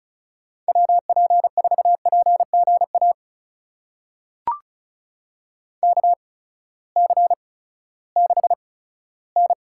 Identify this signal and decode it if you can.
Morse code, heard as a keyed tone, about 34 wpm, spaced as for 17 wpm, 700 Hz. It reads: WP4PGA KC6N
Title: USA Call Signs - 8x Character Spacing - 35wpm